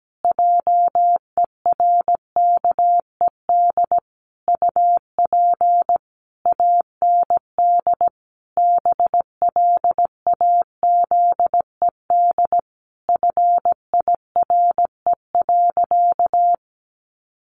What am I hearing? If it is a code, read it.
Morse code, heard as a keyed tone, about 17 wpm, 700 Hz. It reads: JERKED UP AND BLAZED FIRE.
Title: A Sound Of Thunder 17wpm